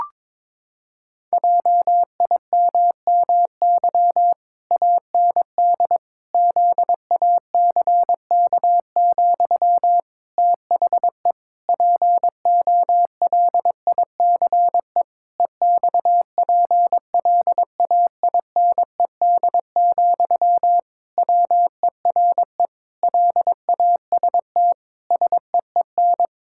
Morse code, heard as a keyed tone, about 22 wpm, 700 Hz. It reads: JIMMY AND ZACK, THE POLICE EXPLAINED, WERE LAST SEEN
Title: Pangram Challenge v2 - 22wpm